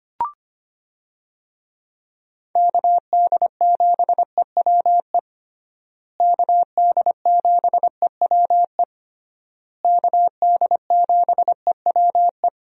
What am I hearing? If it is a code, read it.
Morse code, heard as a keyed tone, about 25 wpm, 700 Hz. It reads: KD7EWE KD7EWE KD7EWE
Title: USA Call Signs - 6 Characters - Repeated 3x - 25wpm